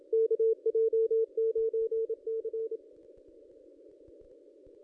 Morse code, heard as a keyed tone, about 28 wpm, 440 Hz. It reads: KJ9C
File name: contests / ss-cw-k5tr-op-n5rz -1500